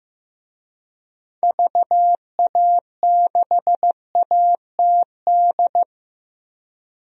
Morse code, heard as a keyed tone, about 15 wpm, 700 Hz. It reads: VA6ATD
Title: DXCC-100 Call Signs - 6 Characters - 15wpm